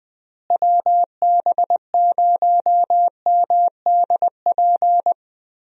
Morse code, heard as a keyed tone, about 21 wpm, 700 Hz. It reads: WB0MDP
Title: USA Call Signs - 6 Characters - Repeated 3x - 20wpm